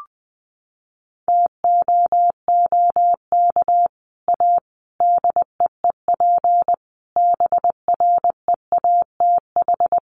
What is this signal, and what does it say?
Morse code, heard as a keyed tone, about 20 wpm, 700 Hz. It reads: TOOK A DEEP BREATH
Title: Sets of 5 Words Encore 20wpm